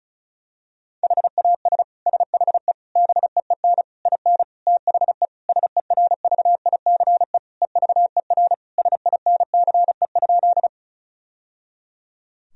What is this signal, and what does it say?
Morse code, heard as a keyed tone, about 34 wpm, 700 Hz. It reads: HAS SHE BEEN IN THE SERVICE EVER SINCE?
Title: Sentences from Top 1000 Words - Review Part 4 - 35wpm